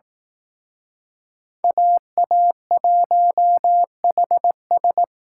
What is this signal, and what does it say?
Morse code, heard as a keyed tone, about 18 wpm, 700 Hz. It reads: AA1HS
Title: Speed Racing - USA 2x2 Call Signs 15wpm